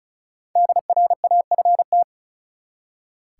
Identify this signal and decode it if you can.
Morse code, heard as a keyed tone, about 34 wpm, 700 Hz. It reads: DRAFT
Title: Five Letter Words - 35wpm